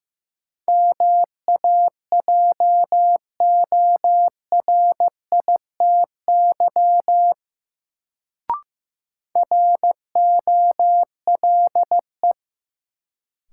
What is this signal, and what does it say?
Morse code, heard as a keyed tone, about 15 wpm, 700 Hz. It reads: MAJORITY ROLE
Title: Sets of 3 Words - New Word Drill - 15wpm